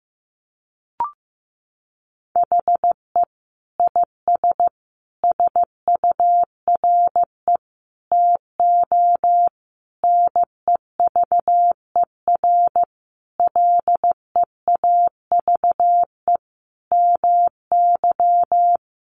Morse code, heard as a keyed tone, about 15 wpm, 700 Hz. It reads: HE IS SURE TO NEVER LEAVE MY
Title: Sentences from Top 700 Words 15wpm